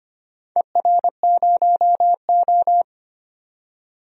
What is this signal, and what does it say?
Morse code, heard as a keyed tone, about 25 wpm, 700 Hz. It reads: ER0O
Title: DXCC-100 Call Signs - 4 Characters - 25wpm